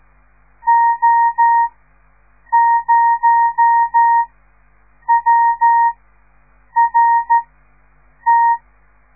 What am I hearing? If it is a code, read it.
Morse code, heard as a keyed tone, about 13 wpm, spaced as for 7 wpm, 930 Hz. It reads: O0WRT
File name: radiomorningstar / morse / les 6